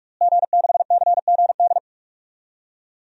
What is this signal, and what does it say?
Morse code, heard as a keyed tone, about 44 wpm, 700 Hz. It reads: G6CGD
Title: DXCC-100 Call Signs - 5 Characters - Repeated 2x - 45wpm